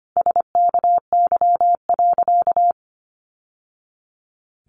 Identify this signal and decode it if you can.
Morse code, heard as a keyed tone, about 25 wpm, 700 Hz. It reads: SKY.
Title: Sentences from Top 1200 Words 25wpm